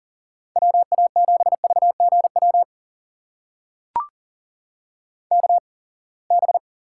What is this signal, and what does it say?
Morse code, heard as a keyed tone, about 40 wpm, spaced as for 28 wpm, 700 Hz. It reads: WA7VGW KB
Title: USA Call Signs - 6 Characters - 8x Character Spacing - 40wpm